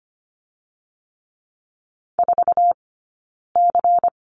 Morse code, heard as a keyed tone, about 25 wpm, 700 Hz. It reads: 4 C
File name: course / 25:10wpm / 092 Lesson 16d - Letter c - Two Characters 25wpm